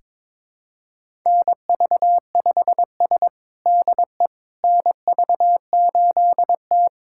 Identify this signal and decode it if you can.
Morse code, heard as a keyed tone, about 22 wpm, 700 Hz. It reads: NV5S DE NV8T
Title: USA Callsign DE Callsign - 4 Characters - 22wpm